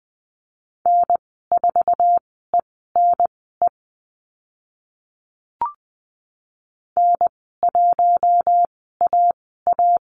Morse code, heard as a keyed tone, about 20 wpm, spaced as for 14 wpm, 700 Hz. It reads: N4ENE N1AA
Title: 076 Lesson 13g - Letter L - Call Signs 20wpm